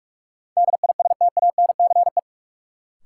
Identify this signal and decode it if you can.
Morse code, heard as a keyed tone, about 47 wpm, 700 Hz. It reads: DISTANCE
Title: Sentences from Top 1200 Words - New Word Drill Alt - 45wpm